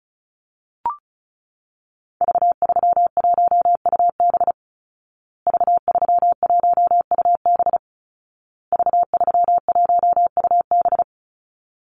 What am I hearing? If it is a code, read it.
Morse code, heard as a keyed tone, about 34 wpm, 700 Hz. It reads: V31UB V31UB V31UB
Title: Call Signs Repeated 2x - 35wpm